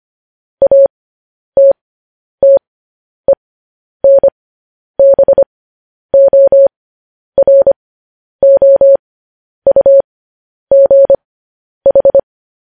Morse code, heard as a keyed tone, about 25 wpm, spaced as for 10 wpm, 550 Hz. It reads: ATTENBOROUGH
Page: news-headlines-in-morse-code-at-10-wpm